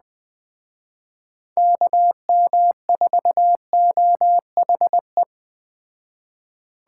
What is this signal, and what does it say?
Morse code, heard as a keyed tone, about 20 wpm, 700 Hz. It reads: KM4OHE
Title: Speed Racing - USA 2x3 Call Signs 17wpm